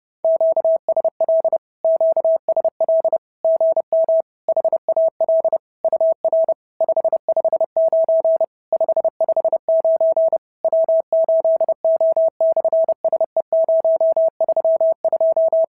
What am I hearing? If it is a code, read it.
Morse code, heard as a keyed tone, about 30 wpm, 690 Hz. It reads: QSL QSL GM HAL UR 559 559 W8O/SE032
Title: SOTA QSO Practice 30wpm